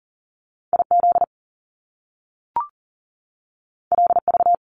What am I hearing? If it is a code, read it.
Morse code, heard as a keyed tone, about 39 wpm, 700 Hz. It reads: IZ LV
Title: Top Two Letter Combinations in Words 40wpm